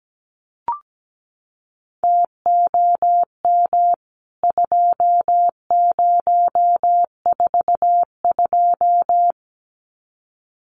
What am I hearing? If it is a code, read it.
Morse code, heard as a keyed tone, about 17 wpm, 700 Hz. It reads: TOM 2042
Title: CWT Exchange 17wpm